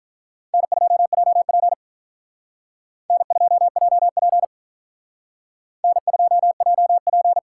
Morse code, heard as a keyed tone, about 43 wpm, 700 Hz. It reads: N2JP N2JP N2JP
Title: Speed Racing - USA 1x2 Call Signs 35wpm